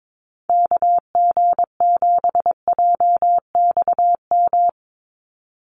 Morse code, heard as a keyed tone, about 22 wpm, 700 Hz. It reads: KG7JXM
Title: USA Call Signs 22wpm